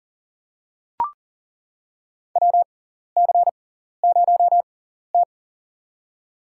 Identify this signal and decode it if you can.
Morse code, heard as a keyed tone, about 40 wpm, spaced as for 14 wpm, 700 Hz. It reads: WC0T
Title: USA Call Signs - 4 Characters - 6x Character Spacing - 40wpm